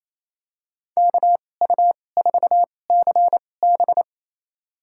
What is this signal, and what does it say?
Morse code, heard as a keyed tone, about 28 wpm, spaced as for 20 wpm, 700 Hz. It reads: KU4CB
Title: USA Call Signs - 5 Characters - 2x Character Spacing - 28wpm